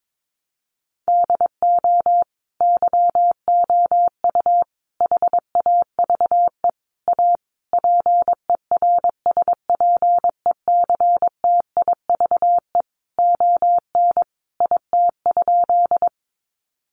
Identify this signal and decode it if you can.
Morse code, heard as a keyed tone, about 22 wpm, 700 Hz. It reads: DO YOU HAVE A PERSPECTIVE ON IT?
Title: Sentences from Top 1200 Words 22wpm